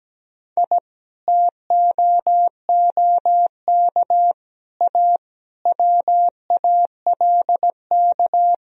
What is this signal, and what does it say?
Morse code, heard as a keyed tone, about 17 wpm, 700 Hz. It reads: I TOOK A WALK
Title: Sentences from Top 1000 Words - Review Part 4 - 17wpm